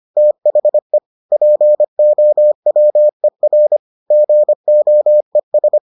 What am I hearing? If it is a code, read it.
Morse code, heard as a keyed tone, about 25 wpm, 600 Hz. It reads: THE POWER GOES